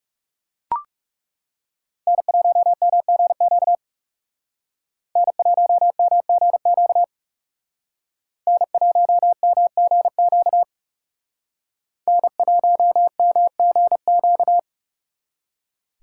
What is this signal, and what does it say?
Morse code, heard as a keyed tone, about 36 wpm, 700 Hz. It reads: N1MGQ N1MGQ N1MGQ N1MGQ
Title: Speed Racing - USA 1x3 Call Signs 30wpm